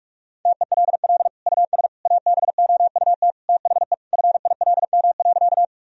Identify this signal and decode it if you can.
Morse code, heard as a keyed tone, about 46 wpm, 700 Hz. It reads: TELL US ABOUT THE FILM.
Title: Sentences from Top 1000 Words - Review Part 3 - 45wpm